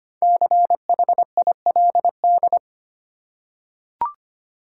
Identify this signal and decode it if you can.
Morse code, heard as a keyed tone, about 25 wpm, 700 Hz. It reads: CHILD
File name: 5 Letter Words - First 3 Letters Given - 25wpm